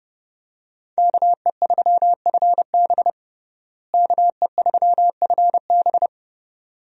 Morse code, heard as a keyed tone, about 30 wpm, 700 Hz. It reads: KE3FB KE3FB
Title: USA Call Signs - 5 Characters - Repeated 2x - 30wpm